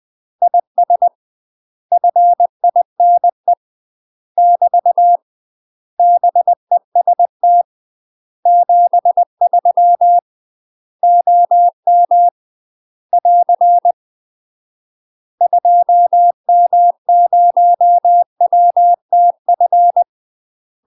Morse code, heard as a keyed tone, about 20 wpm, 700 Hz. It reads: IS FINE <BT> BEST 73 OM <AR> 2M0WTF
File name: UK MASTER QSOs TRAINING/20WPM